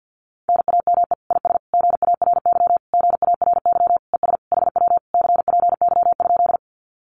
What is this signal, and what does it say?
Morse code, heard as a keyed tone, about 50 wpm, 700 Hz. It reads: NAME IS GARY GARY ES HW CPY?
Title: Common QSO Phrases Encore 50wpm